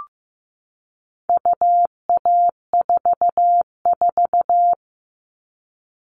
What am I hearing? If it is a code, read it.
Morse code, heard as a keyed tone, about 15 wpm, 700 Hz. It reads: UA44
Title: All Cut Numbers - 4-Digit 15wpm